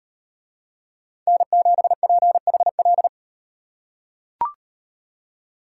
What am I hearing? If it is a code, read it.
Morse code, heard as a keyed tone, about 38 wpm, 700 Hz. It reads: N7PHL